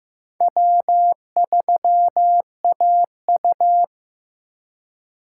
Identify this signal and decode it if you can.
Morse code, heard as a keyed tone, about 15 wpm, 700 Hz. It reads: W3AU
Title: USA Call Signs - 4 Characters - 2x Character Spacing - 15wpm